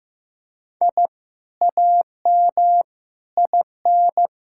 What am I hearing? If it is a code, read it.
Morse code, heard as a keyed tone, about 15 wpm, 700 Hz. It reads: I AM IN
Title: Sentences from Top 1000 Words - Review Part 6 - 15wpm